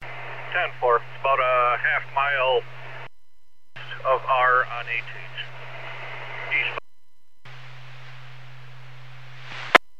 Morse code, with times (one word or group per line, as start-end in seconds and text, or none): none